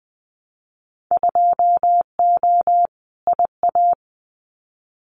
Morse code, heard as a keyed tone, about 20 wpm, 700 Hz. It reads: 2O IA
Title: Winter Field Day Exchanges 20wpm